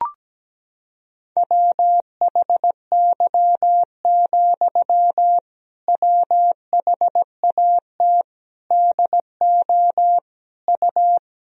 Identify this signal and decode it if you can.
Morse code, heard as a keyed tone, about 17 wpm, 700 Hz. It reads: WHY, WHAT DO U
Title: Sentences from Top 200 Words and Abbreviations 17wpm